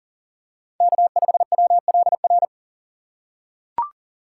Morse code, heard as a keyed tone, about 40 wpm, 700 Hz. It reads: K5WLR
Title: USA Call Signs - 5 Characters - 4x Character Spacing - 40wpm